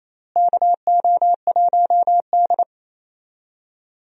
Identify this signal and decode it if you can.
Morse code, heard as a keyed tone, about 28 wpm, 700 Hz. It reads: KO1D
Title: USA Call Signs - 4 Characters - Repeated 3x - 28wpm